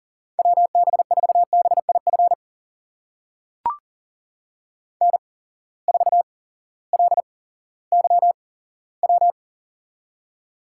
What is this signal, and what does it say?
Morse code, heard as a keyed tone, about 38 wpm, spaced as for 18 wpm, 700 Hz. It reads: WB4BIF N4LYW
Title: USA Call Signs - 8x Character Spacing - 40wpm